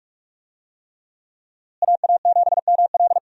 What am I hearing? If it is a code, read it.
Morse code, heard as a keyed tone, about 46 wpm, 700 Hz. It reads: AA7ML
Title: USA Call Signs - 5 Characters - Repeated 3x - 45wpm